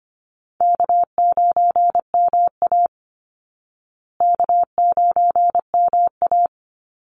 Morse code, heard as a keyed tone, about 25 wpm, 700 Hz. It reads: K9MA K9MA
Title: Easy Call Signs Repeated 2x - 25wpm